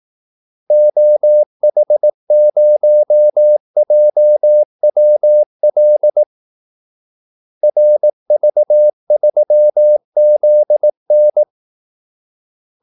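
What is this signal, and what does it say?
Morse code, heard as a keyed tone, about 18 wpm, 600 Hz. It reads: OH0JWL RV3ZN
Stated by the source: Morse call-sign practice at 18 wpm.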